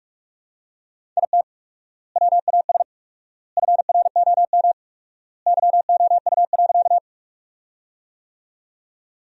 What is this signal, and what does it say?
Morse code, heard as a keyed tone, about 46 wpm, 700 Hz. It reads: IT WAS FROM YOU.
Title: 4x Word-Spacing - Sentences from Top 100 Words - 45wpm